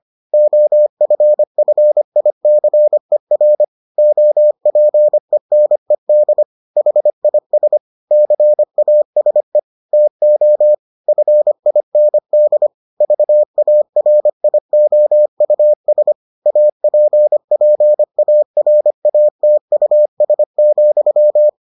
Morse code, heard as a keyed tone, about 25 wpm, 600 Hz. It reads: OFFICER OPENED HIS CASE TO FIND VARIOUS APPARATUS,